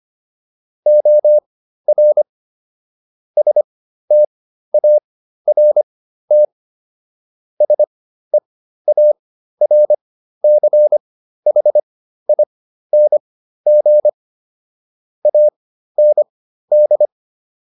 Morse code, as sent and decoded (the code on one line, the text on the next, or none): --- .-. / ... - .- .-. - / ... .- .-. -.-. .... .. -. --. / .- -. -..
OR START SEARCHING AND